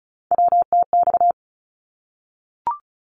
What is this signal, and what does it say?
Morse code, heard as a keyed tone, about 34 wpm, 700 Hz. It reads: WTX
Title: ARRL Sections 35wpm